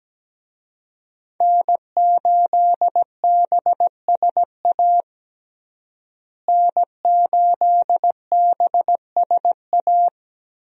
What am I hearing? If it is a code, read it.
Morse code, heard as a keyed tone, about 17 wpm, 700 Hz. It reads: N8BSA N8BSA
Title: USA Call Signs - 5 Characters - Repeated 3x - 17wpm